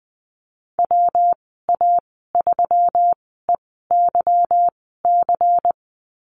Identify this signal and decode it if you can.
Morse code, heard as a keyed tone, about 20 wpm, spaced as for 15 wpm, 700 Hz. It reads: WA3EYC
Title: 167 Lesson 27g - Number 7 - Call Signs 20wpm